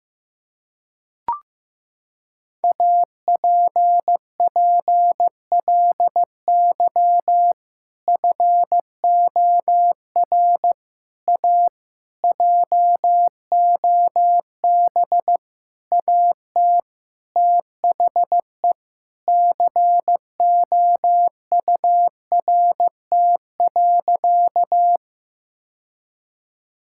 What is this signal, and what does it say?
Morse code, heard as a keyed tone, about 15 wpm, 700 Hz. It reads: APPLY FOR A JOB AT THE COURT.
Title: Sentences from Top 1600 Words 15wpm